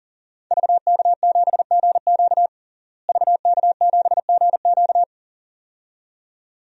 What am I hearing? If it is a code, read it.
Morse code, heard as a keyed tone, about 37 wpm, 700 Hz. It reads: VK7GQ VK7GQ
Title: DXCC-100 Call Signs - 5 Characters - Repeated 2x - 40wpm